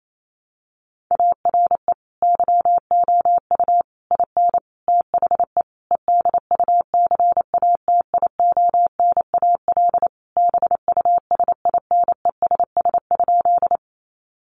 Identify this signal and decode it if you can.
Morse code, heard as a keyed tone, about 28 wpm, 700 Hz. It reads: ARE YOU IN THE EDUCATIONAL BUSINESS?
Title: Sentences from Top 1200 Words 28wpm